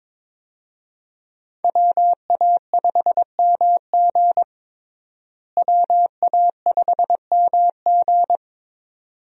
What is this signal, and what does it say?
Morse code, heard as a keyed tone, about 22 wpm, 700 Hz. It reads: WA5MG WA5MG
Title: USA Call Signs - 5 Characters - Repeated 2x - 22wpm